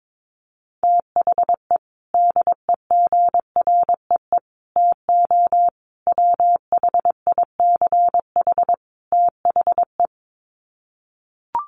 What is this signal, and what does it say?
Morse code, heard as a keyed tone, about 22 wpm, 700 Hz. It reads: THE DEGREE TO WHICH THE
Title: Sets of 5 Words Encore2 22wpm